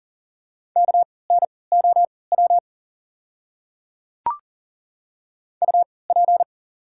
Morse code, heard as a keyed tone, about 40 wpm, spaced as for 23 wpm, 700 Hz. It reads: KNOW UP
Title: 3x Character Spacing - Top 200 Words - 40wpm